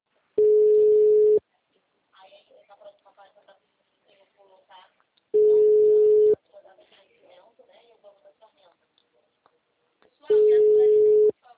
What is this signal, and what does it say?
Ringback tone (Europe, 425 Hz), with speech.
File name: out- -2002- - - .24502